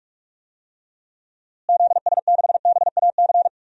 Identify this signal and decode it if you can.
Morse code, heard as a keyed tone, about 46 wpm, 700 Hz. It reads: ZS6BAC
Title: DXCC-100 Call Signs - 6 Characters - Repeated 3x - 45wpm